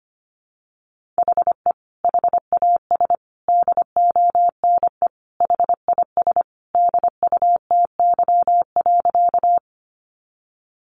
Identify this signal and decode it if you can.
Morse code, heard as a keyed tone, about 25 wpm, 700 Hz. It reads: HE HAS DONE HIS DUTY.
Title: Sentences from Top 1500 Words 25wpm